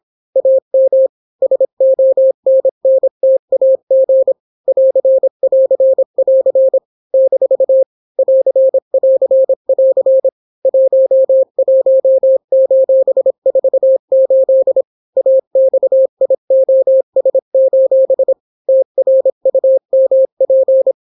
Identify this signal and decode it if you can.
Morse code, heard as a keyed tone, about 26 wpm, 530 Hz. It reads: AM SONNTAG <AR><AR><AR> - <AR><AR><AR> 11:48 AXIOS: TRUMP